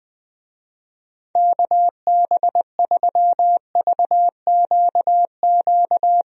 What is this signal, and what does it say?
Morse code, heard as a keyed tone, about 21 wpm, 700 Hz. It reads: KB3VQQ
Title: USA Call Signs - 6 Characters - 20wpm